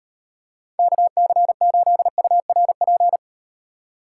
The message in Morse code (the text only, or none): KC8URP